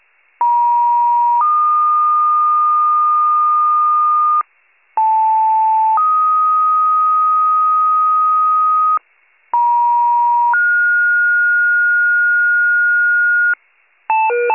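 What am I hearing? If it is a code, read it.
Two-tone page, heard as a one-second tone, then a three-second tone, about 950 then 1220 Hz, about 880 then 1250 Hz, about 950 then 1430 Hz, with noise. It is heard from 0.4 to 13.5 s.
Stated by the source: County Wide Paging Archive fire pager two-tone alert recording.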